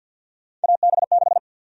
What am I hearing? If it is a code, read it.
Morse code, heard as a keyed tone, about 49 wpm, 700 Hz. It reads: AB6
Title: USA Call Sign Prefixes - Repeated 2x - 50wpm